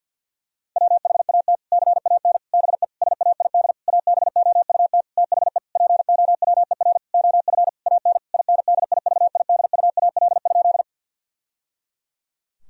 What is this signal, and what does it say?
Morse code, heard as a keyed tone, about 50 wpm, 700 Hz. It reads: WHAT CAN BE SAID ABOUT THE POWER OF AN INDIVIDUAL?
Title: Sentences from Top 1000 Words - Review Part 4 - 50wpm